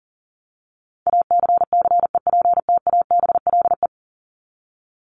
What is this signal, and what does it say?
Morse code, heard as a keyed tone, about 40 wpm, 700 Hz. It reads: ACCEPTABLE
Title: Top 1501-1750 Words 40wpm